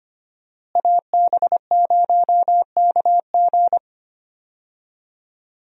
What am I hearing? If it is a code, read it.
Morse code, heard as a keyed tone, about 25 wpm, 700 Hz. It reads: AB0KG
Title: USA Call Signs - 5 Characters - Repeated 3x - 25wpm